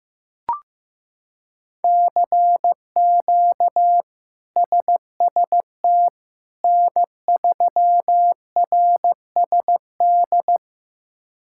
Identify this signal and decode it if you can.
Morse code, heard as a keyed tone, about 15 wpm, 700 Hz. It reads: CQ SST N3RSD
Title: K1USN SST QSO Practice - 15wpm